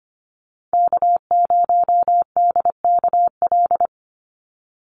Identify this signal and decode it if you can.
Morse code, heard as a keyed tone, about 25 wpm, 700 Hz. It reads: K0DKL
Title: USA Call Signs 25wpm